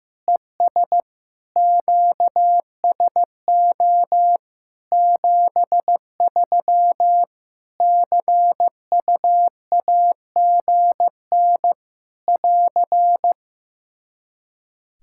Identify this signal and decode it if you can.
Morse code, heard as a keyed tone, about 15 wpm, 700 Hz. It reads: ES QSO 73 CUAGN <AR>
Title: Practice QSO - Exchange 4 - Ending - 15wpm